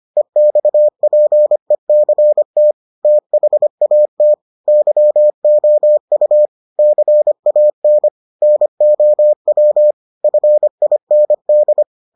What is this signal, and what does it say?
Morse code, heard as a keyed tone, about 25 wpm, 600 Hz. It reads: EXPECT THAT YOU CAN NOW FIND